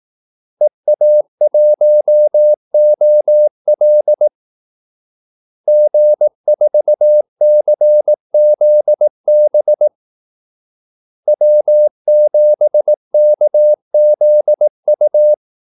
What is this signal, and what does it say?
Morse code, heard as a keyed tone, about 18 wpm, 600 Hz. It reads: EA1OL G4CZB W7KZU
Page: Kaldesignaler 18 wpm